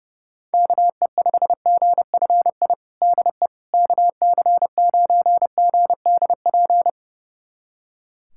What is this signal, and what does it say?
Morse code, heard as a keyed tone, about 30 wpm, 700 Hz. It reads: KE5GFI DE KC9GDP
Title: USA Callsign DE Callsign 30wpm